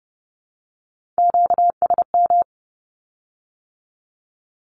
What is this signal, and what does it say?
Morse code, heard as a keyed tone, about 30 wpm, 700 Hz. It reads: QSM
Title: Q-Codes with Spoken Meaning - 30wpm